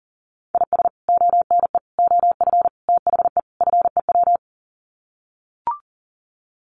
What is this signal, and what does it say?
Morse code, heard as a keyed tone, about 40 wpm, 700 Hz. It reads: IS ONE OF THE FEW
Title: Sets of 5 Words 40wpm